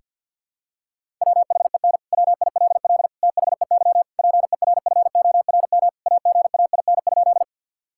Morse code, heard as a keyed tone, about 52 wpm, 700 Hz. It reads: WHEN WILL THEY PERFORM AGAIN?
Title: Sentences from Top 1000 Words - Review Part 5 - 50wpm